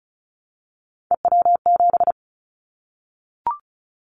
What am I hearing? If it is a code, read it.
Morse code, heard as a keyed tone, about 36 wpm, 700 Hz. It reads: EW7
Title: Callsign Prefixes for Top 50 DXCC - 35wpm